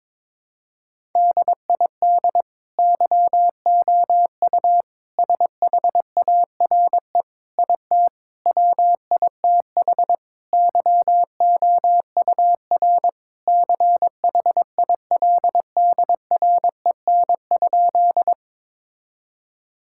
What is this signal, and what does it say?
Morse code, heard as a keyed tone, about 22 wpm, 700 Hz. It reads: DID YOU SHARE IT WITH YOUR CHILDREN?
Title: Sentences from Top 1000 Words - Review Part 6 - 22wpm